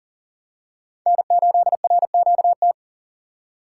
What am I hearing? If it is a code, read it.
Morse code, heard as a keyed tone, about 40 wpm, 700 Hz. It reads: N8RQT
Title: USA Call Signs - 5 Characters - 2x Character Spacing - 40wpm